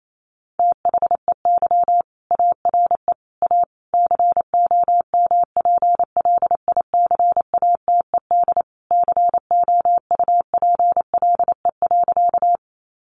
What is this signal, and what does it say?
Morse code, heard as a keyed tone, about 28 wpm, 700 Hz. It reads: THEY ARE A COMPLICATED COUPLE.
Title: Sentences from Top 1200 Words 28wpm